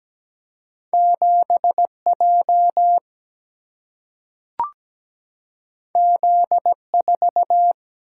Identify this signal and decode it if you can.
Morse code, heard as a keyed tone, about 17 wpm, 700 Hz. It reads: 7J Z4
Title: Single-Letter Single-Number 17wpm